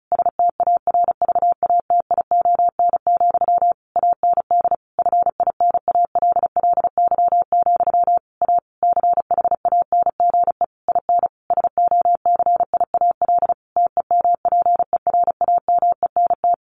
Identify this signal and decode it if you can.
Morse code, heard as a keyed tone, about 34 wpm, 700 Hz. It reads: STARVATION, AND FINALLY, A CHANGE IN SOCIAL TEMPERAMENT
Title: A Sound Of Thunder 35wpm